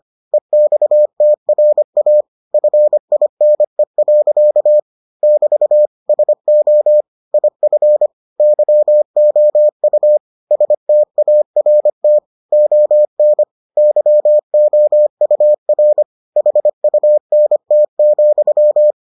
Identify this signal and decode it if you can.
Morse code, heard as a keyed tone, about 25 wpm, 600 Hz. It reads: EXTRA FINE. <BT> SO IF YOU START ON YOUR HUNT,